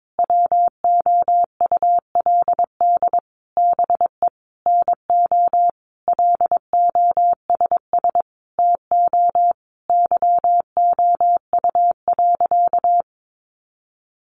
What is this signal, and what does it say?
Morse code, heard as a keyed tone, about 22 wpm, 700 Hz. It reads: WOULD BE NO LOSS TO YOU.
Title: Sentences from Top 1400 Words 22wpm